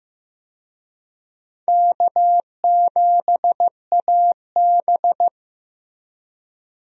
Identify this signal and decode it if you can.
Morse code, heard as a keyed tone, about 15 wpm, 700 Hz. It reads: K7AB
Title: USA Call Signs - 4 Characters - Repeated 2x - 15wpm